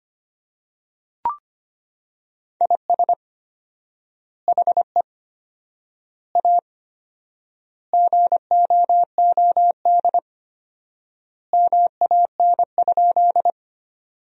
Morse code, heard as a keyed tone, about 25 wpm, 700 Hz. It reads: IS HE A GOOD MAN?
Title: 4x Word-Spacing - Sentences from Top 200 Words - 25wpm